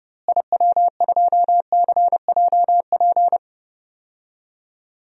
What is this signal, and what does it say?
Morse code, heard as a keyed tone, about 30 wpm, 700 Hz. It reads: IW2CJP
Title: DXCC-100 Call Signs - 6 Characters - Repeated 3x - 30wpm